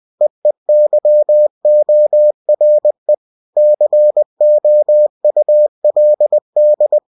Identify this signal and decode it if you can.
Morse code, heard as a keyed tone, about 20 wpm, 600 Hz. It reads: EEYORE COULD
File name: cw-books / winnie-the-pooh / winnie-20wpm-page-0091